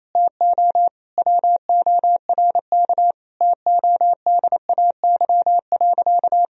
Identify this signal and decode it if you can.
Morse code, heard as a keyed tone, about 28 wpm, 700 Hz. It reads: TO WORK TODAY.
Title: Sentences from Top 1000 Words - Review Part 4 - 28wpm